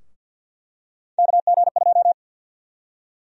Morse code, heard as a keyed tone, about 48 wpm, 700 Hz. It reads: KG2
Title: USA Call Sign Prefixes - 50wpm